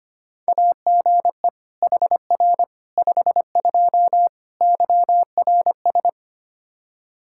Morse code, heard as a keyed tone, about 25 wpm, 700 Hz. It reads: AGE HR 52 YRS
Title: Practice QSO - Exchange 3 - More Info - 25wpm